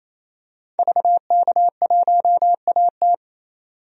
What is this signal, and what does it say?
Morse code, heard as a keyed tone, about 28 wpm, 700 Hz. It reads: VK1AT
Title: DXCC-100 Call Signs - 5 Characters - 28wpm